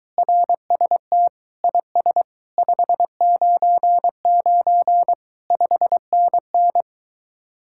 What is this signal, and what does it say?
Morse code, heard as a keyed tone, about 23 wpm, 700 Hz. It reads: RST IS 599 5NN